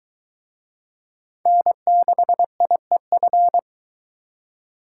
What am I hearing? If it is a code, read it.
Morse code, heard as a keyed tone, about 23 wpm, 700 Hz. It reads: N6IEF